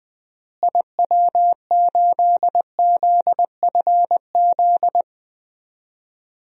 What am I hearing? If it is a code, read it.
Morse code, heard as a keyed tone, about 20 wpm, 700 Hz. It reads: IW8ZFZ
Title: DXCC-100 Call Signs - 6 Characters - Repeated 3x - 20wpm